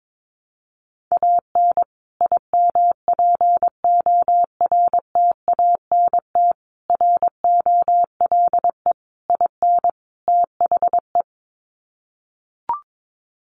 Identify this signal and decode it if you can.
Morse code, heard as a keyed tone, about 22 wpm, 700 Hz. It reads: AN IMPORTANT ROLE IN THE
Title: Sets of 5 Words Encore2 22wpm